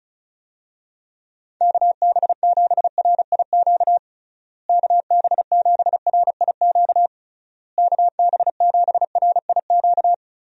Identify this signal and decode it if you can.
Morse code, heard as a keyed tone, about 36 wpm, 700 Hz. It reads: KB7RIQ KB7RIQ KB7RIQ
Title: USA Call Signs - 6 Characters - Repeated 3x - 35wpm